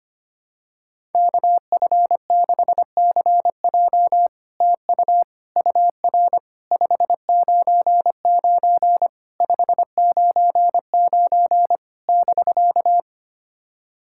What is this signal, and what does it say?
Morse code, heard as a keyed tone, about 25 wpm, 700 Hz. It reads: KF6CJ TU UR 599 599 <BK>
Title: SOTA QSO Practice 25wpm